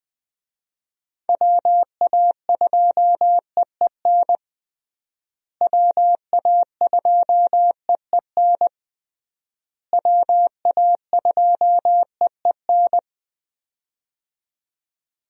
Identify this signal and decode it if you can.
Morse code, heard as a keyed tone, about 21 wpm, 700 Hz. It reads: WA2EEN WA2EEN WA2EEN
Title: USA Call Signs - 6 Characters - Repeated 3x - 20wpm